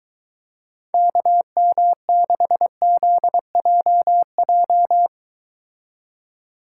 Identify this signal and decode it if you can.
Morse code, heard as a keyed tone, about 23 wpm, 700 Hz. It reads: KM6ZJJ